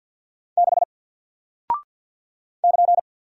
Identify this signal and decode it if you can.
Morse code, heard as a keyed tone, about 48 wpm, 700 Hz. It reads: 6 <KN>
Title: Single Character (Letter:Number:Punctuation:Prosign) Alt 50wpm